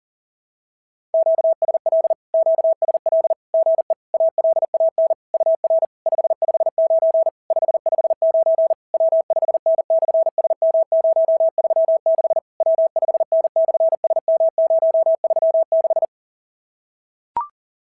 Morse code, heard as a keyed tone, about 40 wpm, 660 Hz. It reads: QSL QSL GE ALAN UR 559 559 W5N/SM036 W5N/SM036
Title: SOTA QSO Practice 40wpm